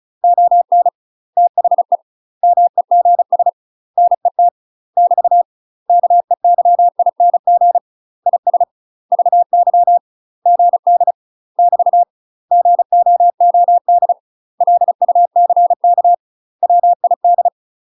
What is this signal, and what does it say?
Morse code, heard as a keyed tone, about 36 wpm, 700 Hz. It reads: ON THE MEGS NET <BT> KEYING IS VY GD <BT> GOOD LUCK WID
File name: UK MASTER QSOs TRAINING / 35WPM / UK MASTER QSOs 0011